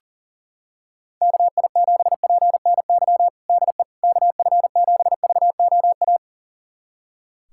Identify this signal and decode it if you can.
Morse code, heard as a keyed tone, about 40 wpm, 700 Hz. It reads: KI7PNY DE KF7VOA
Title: USA Callsign DE Callsign - 6 Characters - 40wpm